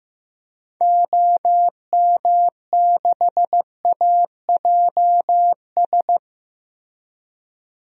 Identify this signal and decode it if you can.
Morse code, heard as a keyed tone, about 16 wpm, 700 Hz. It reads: OM6AJS
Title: DXCC-100 Call Signs - 6 Characters - Repeated 2x - 15wpm